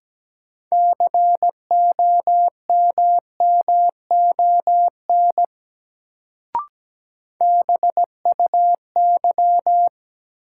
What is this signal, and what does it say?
Morse code, heard as a keyed tone, about 17 wpm, 700 Hz. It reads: COMMON BUY
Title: Sentences from Top 700 Words - New Word Drill - 17wpm